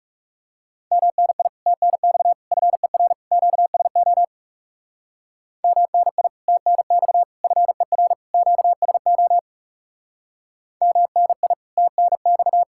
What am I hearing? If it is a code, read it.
Morse code, heard as a keyed tone, about 36 wpm, 700 Hz. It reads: MNI TNX FER QSO MNI TNX FER QSO MNI TNX
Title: Speed-Racing Common QSO Phrases - 30wpm